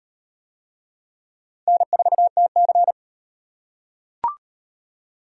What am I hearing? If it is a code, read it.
Morse code, heard as a keyed tone, about 38 wpm, 700 Hz. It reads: N4TC